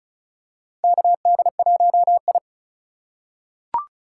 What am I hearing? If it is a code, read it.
Morse code, heard as a keyed tone, about 36 wpm, 700 Hz. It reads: KD1I
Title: USA Call Signs - 4 Characters - 4x Character Spacing - 35wpm